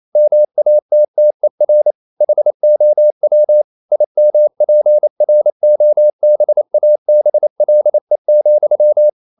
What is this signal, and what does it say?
Morse code, heard as a keyed tone, about 28 wpm, 600 Hz. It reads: MATTER HOW IMPROBABLE,
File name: CW / Practice / EnglishQuotations0228